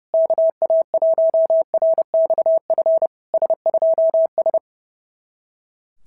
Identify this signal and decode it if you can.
Morse code, heard as a keyed tone, about 30 wpm, 650 Hz. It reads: KA1RXF S2S
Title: SOTA QSO Practice 30wpm